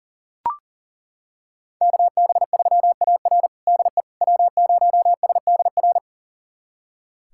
Morse code, heard as a keyed tone, about 40 wpm, 700 Hz. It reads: KB3AR DE W0SDR
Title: USA Callsign DE Callsign - 5 Characters - 40wpm